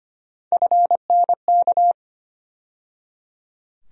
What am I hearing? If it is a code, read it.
Morse code, heard as a keyed tone, about 25 wpm, 700 Hz. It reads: FNK
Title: USA Call Sign Postfixes - 3 Letters 25wpm